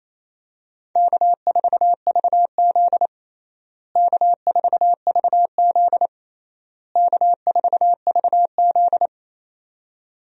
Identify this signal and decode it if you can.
Morse code, heard as a keyed tone, about 28 wpm, 700 Hz. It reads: K4VZ K4VZ K4VZ
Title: USA Call Signs - 4 Characters - Repeated 3x - 28wpm